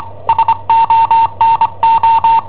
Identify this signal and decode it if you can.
Morse code, heard as a keyed tone, about 25 wpm, 910 Hz. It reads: SONO
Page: Cassetta esercizi preparata da I7OHP per IW5ELP